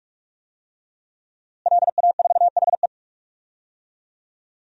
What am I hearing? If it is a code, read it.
Morse code, heard as a keyed tone, about 46 wpm, 700 Hz. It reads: LA4HE
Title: DXCC-100 Call Signs - 5 Characters - Repeated 3x - 45wpm